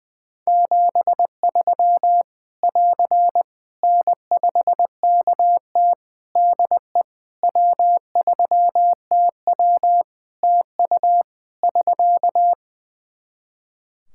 Morse code, heard as a keyed tone, about 20 wpm, 700 Hz. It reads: 73 <AR> N5KT DE W3TW TU <SK>
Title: Practice QSO - Exchange 4 - Ending - 20wpm